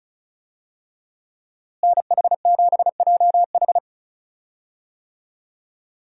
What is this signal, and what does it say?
Morse code, heard as a keyed tone, about 36 wpm, 700 Hz. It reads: NH7JH
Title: USA Call Signs - 5 Characters - Repeated 2x - 35wpm